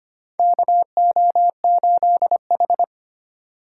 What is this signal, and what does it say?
Morse code, heard as a keyed tone, about 25 wpm, 700 Hz. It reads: KO8H
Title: USA Call Signs - 4 Characters - 2x Character Spacing - 25wpm